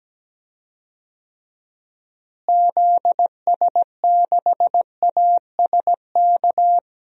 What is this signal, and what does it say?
Morse code, heard as a keyed tone, about 17 wpm, 700 Hz. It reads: ZS6ASK
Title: DXCC-100 Call Signs - 6 Characters - Repeated 3x - 17wpm